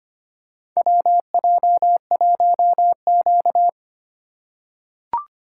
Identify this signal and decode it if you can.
Morse code, heard as a keyed tone, about 25 wpm, 700 Hz. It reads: WJ1Q
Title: USA Call Signs - 4 Characters - 4x Character Spacing - 25wpm